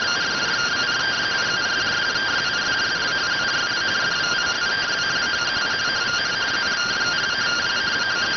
Two-tone teleprinter signal, about 170 Hz shift.